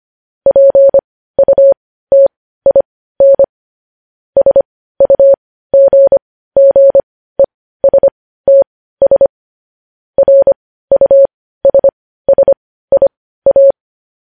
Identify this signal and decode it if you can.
Morse code, heard as a keyed tone, about 25 wpm, spaced as for 15 wpm, 550 Hz. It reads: PUTIN SUGGESTS RUSSIA
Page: news headlines in morse code at 15 wpm